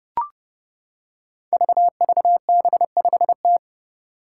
Morse code, heard as a keyed tone, about 29 wpm, 700 Hz. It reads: VVB5T
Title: Five Challenging Letter-Number Combinations 30wpm